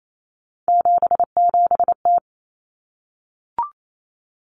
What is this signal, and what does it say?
Morse code, heard as a keyed tone, about 28 wpm, 700 Hz. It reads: 77T
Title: Common Cut Numbers - 3-Digit 28wpm